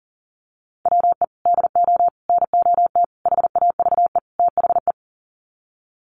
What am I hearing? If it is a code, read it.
Morse code, heard as a keyed tone, about 40 wpm, 700 Hz. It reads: WE DO NOT HAVE THE
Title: Sets of 5 Words 40wpm